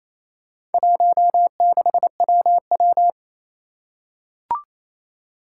Morse code, heard as a keyed tone, about 27 wpm, 700 Hz. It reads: 16WW
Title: Four Challenging Letter-Number Combinations 28wpm